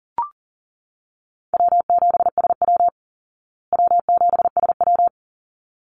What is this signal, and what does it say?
Morse code, heard as a keyed tone, about 40 wpm, 700 Hz. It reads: W7SW W7SW
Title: Easy Call Signs Repeated 2x - 40wpm